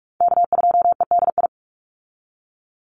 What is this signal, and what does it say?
Morse code, heard as a keyed tone, about 44 wpm, 700 Hz. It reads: K2EDI
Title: USA Call Signs 45wpm